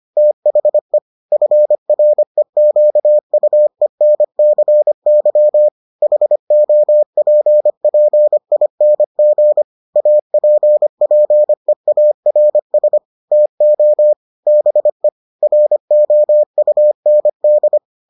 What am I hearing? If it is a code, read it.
Morse code, heard as a keyed tone, about 25 wpm, 600 Hz. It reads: THE FREQUENCY HOPPING APPEARS TO BE ROUND